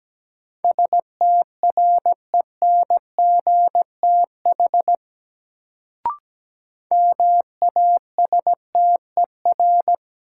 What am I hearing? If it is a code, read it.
Morse code, heard as a keyed tone, about 17 wpm, 700 Hz. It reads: STRENGTH MASTER
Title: Sentences from Top 1300 Words - New Word Drill Alt - 17wpm